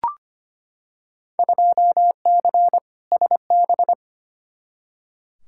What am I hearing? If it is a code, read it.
Morse code, heard as a keyed tone, about 25 wpm, 700 Hz. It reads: 2C SB
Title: Field Day Exchanges Encore - 25wpm